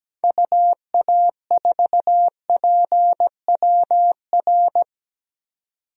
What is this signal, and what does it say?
Morse code, heard as a keyed tone, about 17 wpm, 700 Hz. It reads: UA4PWR
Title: DXCC-100 Call Signs - 6 Characters - 17wpm